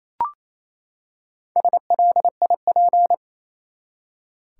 Morse code, heard as a keyed tone, about 28 wpm, 700 Hz. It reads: SLIP
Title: Four Letter Words - 28wpm